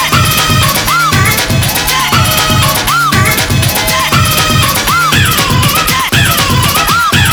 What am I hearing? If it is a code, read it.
There is no code.